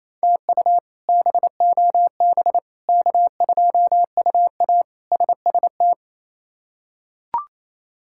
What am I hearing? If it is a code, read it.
Morse code, heard as a keyed tone, about 28 wpm, 700 Hz. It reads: TU BOB K2UA SST
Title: K1USN SST QSO Practice - 28wpm